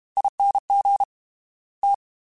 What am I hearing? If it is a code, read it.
Morse code, heard as a keyed tone, about 32 wpm, 800 Hz. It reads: ING T
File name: Special INGs / Special INGs-mp3s / Fast ing @30